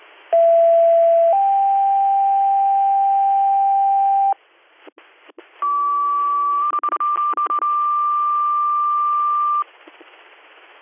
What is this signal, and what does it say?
Two-tone page, heard as a one-second tone, then a three-second tone, about 670 then 790 Hz, about 1150 then 1180 Hz. It is heard from 0.3 to 9.6 s.